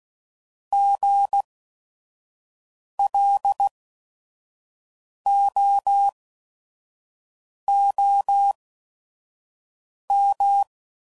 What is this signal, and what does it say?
Morse code, heard as a keyed tone, about 16 wpm, spaced as for 5 wpm, 800 Hz. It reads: GLOOM
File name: Prefixes / By First Letter / Gs